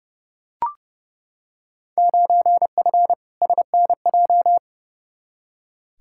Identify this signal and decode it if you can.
Morse code, heard as a keyed tone, about 32 wpm, 700 Hz. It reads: 9F SNJ
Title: Field Day Exchanges Encore - 30wpm